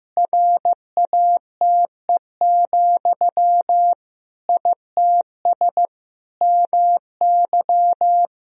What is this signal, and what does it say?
Morse code, heard as a keyed tone, about 15 wpm, 700 Hz. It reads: RATE, ITS MY
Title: Sentences from Top 1000 Words - Review Part 4 - 15wpm